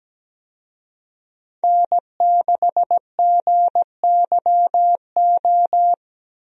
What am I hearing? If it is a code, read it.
Morse code, heard as a keyed tone, about 17 wpm, 700 Hz. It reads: N6GYO
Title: USA Call Signs - 5 Characters - Repeated 3x - 17wpm